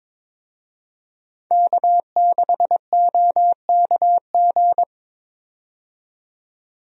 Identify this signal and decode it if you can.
Morse code, heard as a keyed tone, about 22 wpm, 700 Hz. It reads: K6OKG
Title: USA Call Signs - 5 Characters - 22wpm